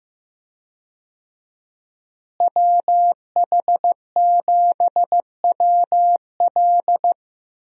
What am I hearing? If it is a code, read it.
Morse code, heard as a keyed tone, about 16 wpm, 700 Hz. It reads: WH7WL
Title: USA Call Signs - 5 Characters - Repeated 3x - 15wpm